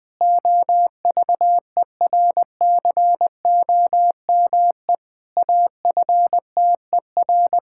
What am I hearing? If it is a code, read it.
Morse code, heard as a keyed tone, about 20 wpm, 700 Hz. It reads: OVERCOME AFTER
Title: Pangram Challenge v2 - 20wpm